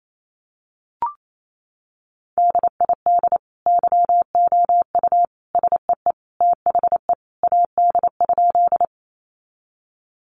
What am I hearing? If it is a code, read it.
Morse code, heard as a keyed tone, about 28 wpm, 700 Hz. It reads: DID YOU SEE THE AD?
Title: Sentences from Top 1300 Words 28wpm